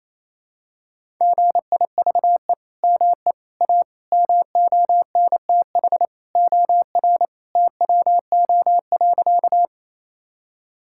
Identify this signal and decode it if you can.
Morse code, heard as a keyed tone, about 28 wpm, 700 Hz. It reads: GIVE ME A MONTH OR TWO.
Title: Sentences from Top 1000 Words - Review Part 5 - 28wpm